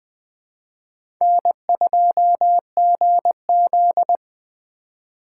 Morse code, heard as a keyed tone, about 20 wpm, 700 Hz. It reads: N2GZ
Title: USA Call Signs - 4 Characters - 4x Character Spacing - 20wpm